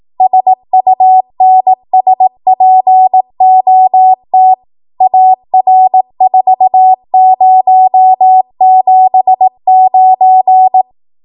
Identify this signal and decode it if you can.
Morse code, heard as a keyed tone, about 18 wpm, 750 Hz. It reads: SUNSPOT AR4079
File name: Morse / 18WPM